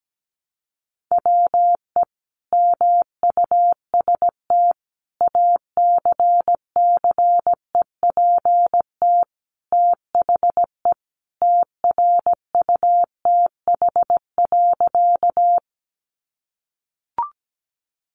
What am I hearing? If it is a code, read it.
Morse code, heard as a keyed tone, about 17 wpm, 700 Hz. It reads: WE MUST ACCEPT THE TRUTH.
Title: Sentences from Top 1300 Words 17wpm